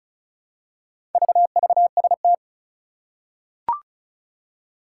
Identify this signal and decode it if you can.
Morse code, heard as a keyed tone, about 36 wpm, 700 Hz. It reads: VVST
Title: Four Challenging Letter-Number Combinations 35wpm